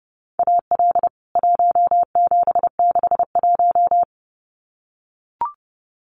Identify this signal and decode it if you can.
Morse code, heard as a keyed tone, about 30 wpm, 700 Hz. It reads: AL 1761
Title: CWT Exchange 30wpm